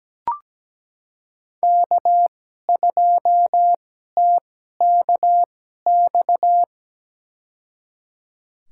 Morse code, heard as a keyed tone, about 17 wpm, spaced as for 12 wpm, 700 Hz. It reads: K2TKX
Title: USA Call Signs - 5 Characters - 2x Character Spacing - 17wpm